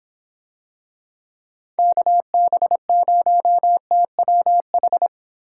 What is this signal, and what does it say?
Morse code, heard as a keyed tone, about 26 wpm, 700 Hz. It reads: KB0TWH